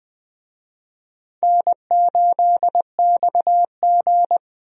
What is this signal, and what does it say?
Morse code, heard as a keyed tone, about 20 wpm, 700 Hz. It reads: N8XG
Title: USA Call Signs - 4 Characters - 20wpm